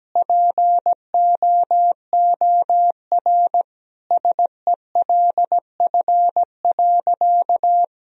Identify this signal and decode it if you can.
Morse code, heard as a keyed tone, about 17 wpm, 700 Hz. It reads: POOR SELF.
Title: Sentences from Top 1000 Words - Review Part 1 - 17wpm